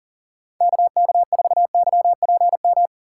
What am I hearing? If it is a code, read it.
Morse code, heard as a keyed tone, about 40 wpm, 700 Hz. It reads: KK4YPM
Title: USA Call Signs - 6x Character Spacing - 40wpm